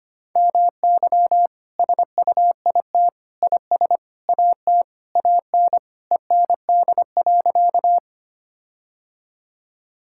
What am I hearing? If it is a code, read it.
Morse code, heard as a keyed tone, about 25 wpm, 700 Hz. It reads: MY SUIT IS AT AN END.
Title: Sentences from Top 1600 Words 25wpm